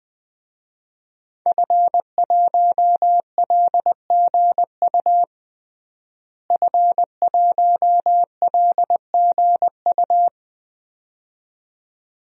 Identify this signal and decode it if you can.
Morse code, heard as a keyed tone, about 20 wpm, 700 Hz. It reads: F1LGU F1LGU
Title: DXCC-100 Call Signs - 5 Characters - Repeated 2x - 20wpm